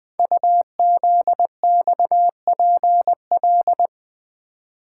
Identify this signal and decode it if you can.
Morse code, heard as a keyed tone, about 20 wpm, 700 Hz. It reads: UZXPL
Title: Code Groups - 5 Letters - 20wpm